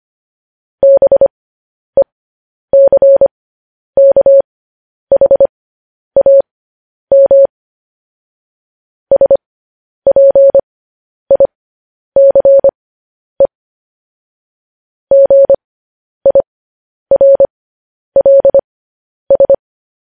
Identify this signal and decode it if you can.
Morse code, heard as a keyed tone, about 25 wpm, spaced as for 10 wpm, 550 Hz. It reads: BECKHAM SPICE GIRLS
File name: podcast / News headlines-1x-25x10